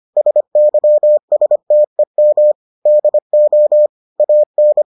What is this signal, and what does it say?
Morse code, heard as a keyed tone, about 25 wpm, 600 Hz. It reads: SYSTEM DO AN